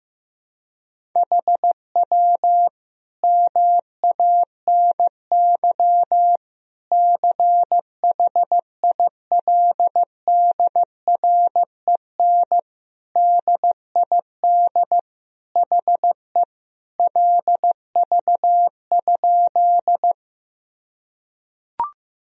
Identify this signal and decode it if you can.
Morse code, heard as a keyed tone, about 15 wpm, 700 Hz. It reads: HW MANY CHILDREN DID HE LV?
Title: Sentences from Top 400 Words and Abbreviations 15wpm